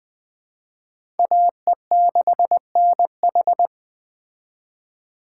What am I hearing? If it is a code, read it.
Morse code, heard as a keyed tone, about 21 wpm, 700 Hz. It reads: AE6NH